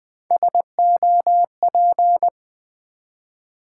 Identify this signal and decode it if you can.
Morse code, heard as a keyed tone, about 20 wpm, 700 Hz. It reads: SOP
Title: Build A Word - 3 Letters At A Time - 9-Letter Words - 20wpm